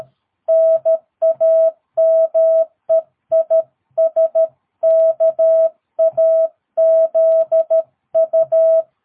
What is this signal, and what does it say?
Morse code, heard as a keyed tone, about 13 wpm, 650 Hz. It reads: NAME IS KAZU
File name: jh8cbh / 2022cw